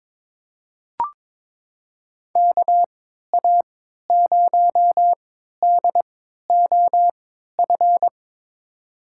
Morse code, heard as a keyed tone, about 22 wpm, spaced as for 13 wpm, 700 Hz. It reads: KA0DOF
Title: USA Call Signs - 3x Character Spacing - 22wpm